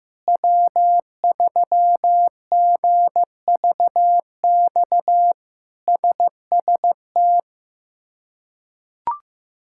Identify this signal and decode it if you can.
Morse code, heard as a keyed tone, about 15 wpm, 700 Hz. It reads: W3GVX SST
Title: K1USN SST QSO Practice - 15wpm